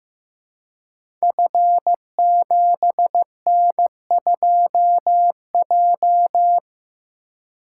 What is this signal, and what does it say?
Morse code, heard as a keyed tone, about 15 wpm, 700 Hz. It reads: F7N2J
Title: Code Groups - 5 Letter:Numbers - 15wpm